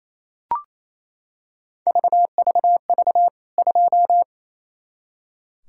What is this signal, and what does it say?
Morse code, heard as a keyed tone, about 28 wpm, 700 Hz. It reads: VVV 2
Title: K1USN SST QSO Practice 28wpm